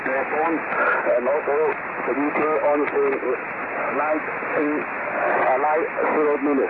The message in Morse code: none

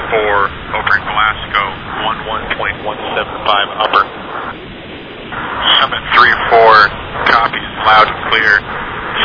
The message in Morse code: none

